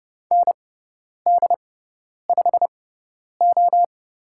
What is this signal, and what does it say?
Morse code, heard as a keyed tone, about 30 wpm, spaced as for 10 wpm, 690 Hz. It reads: ND5O
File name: Lesson 18g - Letter m - Call Signs 30wpm